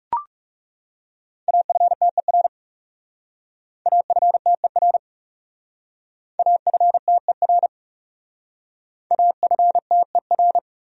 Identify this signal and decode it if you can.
Morse code, heard as a keyed tone, about 38 wpm, 700 Hz. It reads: AFTER AFTER AFTER AFTER